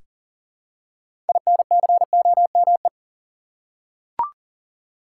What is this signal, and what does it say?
Morse code, heard as a keyed tone, about 37 wpm, 700 Hz. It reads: INCOME
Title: Six Letter Words - 40wpm